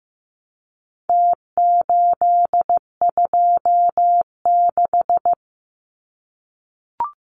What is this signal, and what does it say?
Morse code, heard as a keyed tone, about 15 wpm, 700 Hz. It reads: T826
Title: Numbers - Contest Style - 4-Digit 15wpm